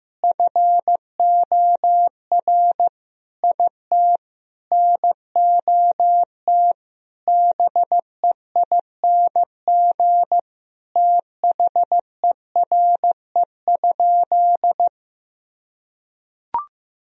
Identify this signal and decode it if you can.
Morse code, heard as a keyed tone, about 15 wpm, 700 Hz. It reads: FOR IT NOT BEING THERE?
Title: Sentences from Top 1000 Words - Review Part 3 - 15wpm